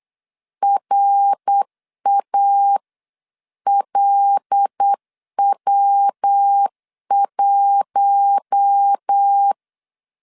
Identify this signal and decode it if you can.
Morse code, heard as a keyed tone, about 8 wpm, 800 Hz. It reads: RALW1
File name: CW / DET / 1 6 1500C S B 8wpm 5142